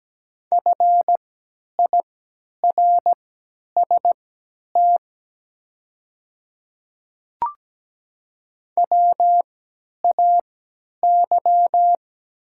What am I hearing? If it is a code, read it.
Morse code, heard as a keyed tone, about 17 wpm, spaced as for 10 wpm, 700 Hz. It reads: FIRST WAY
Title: 3x Character Spacing - Top 300 Words - 17wpm